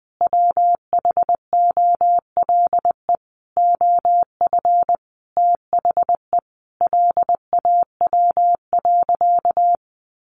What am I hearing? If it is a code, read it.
Morse code, heard as a keyed tone, about 20 wpm, 700 Hz. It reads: WHOLE OF THE LAW.
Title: Sentences from Top 1400 Words 20wpm